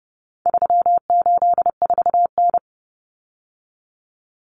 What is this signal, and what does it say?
Morse code, heard as a keyed tone, about 30 wpm, 700 Hz. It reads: 384N
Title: Common Cut Numbers - 4-Digit 30wpm